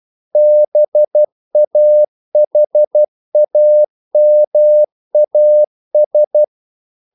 Morse code, heard as a keyed tone, about 12 wpm, 600 Hz. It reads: BAHAMAS
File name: CW / Practice / NationalCapitals12